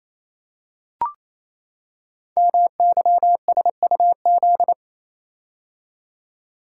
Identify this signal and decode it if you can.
Morse code, heard as a keyed tone, about 29 wpm, 700 Hz. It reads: MYSUZ
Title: Code Groups - 5 Letters - 28wpm